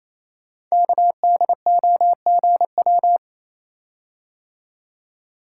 Morse code, heard as a keyed tone, about 29 wpm, 700 Hz. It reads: KDOGW
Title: Code Groups - 5 Letters - 28wpm